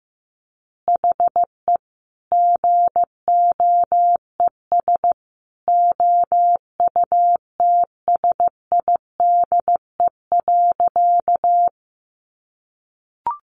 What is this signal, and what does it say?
Morse code, heard as a keyed tone, about 15 wpm, 700 Hz. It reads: HE GOES OUTSIDE.
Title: Additional Easy Phrases 15wpm